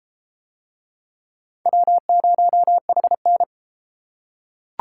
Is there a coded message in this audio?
Yes: Morse code from 1.7 to 3.4 s, heard as a keyed tone, about 33 wpm, 700 Hz. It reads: W0HN